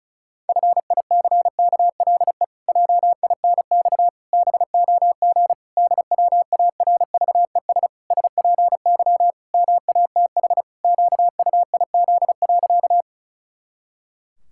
Morse code, heard as a keyed tone, about 36 wpm, 700 Hz. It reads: FICKLE JINX BOG DWARVES SPY MATH QUIZ.
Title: Pangram Challenge v2 - 35wpm